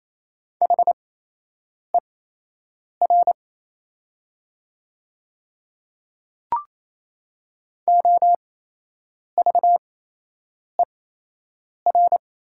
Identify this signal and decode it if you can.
Morse code, heard as a keyed tone, about 28 wpm, spaced as for 8 wpm, 700 Hz. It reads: HER OVER
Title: 8x Character Spacing - Top 100 Words - 28wpm